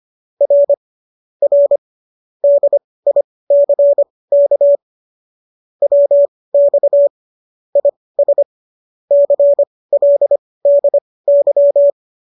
Morse code, heard as a keyed tone, about 25 wpm, spaced as for 18 wpm, 570 Hz. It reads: R R DICK WX IS CLDY